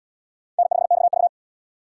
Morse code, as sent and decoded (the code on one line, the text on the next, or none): -. ..... -..- ...-
N5XV